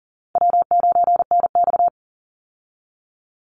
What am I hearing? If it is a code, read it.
Morse code, heard as a keyed tone, about 40 wpm, 700 Hz. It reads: W9NX
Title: Easy Call Signs Repeated 2x - 40wpm